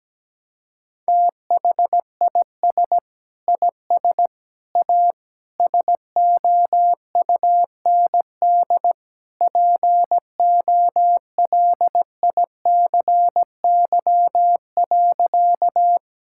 Morse code, heard as a keyed tone, about 17 wpm, 700 Hz. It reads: THIS IS A SOUND POLICY.
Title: Sentences from Top 1000 Words - Review Part 5 - 17wpm